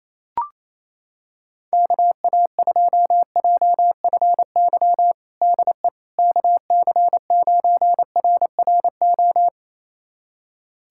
Morse code, heard as a keyed tone, about 28 wpm, 700 Hz. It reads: KA2JFY DE KC9RRO
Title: USA Callsign DE Callsign - 6 Characters - 28wpm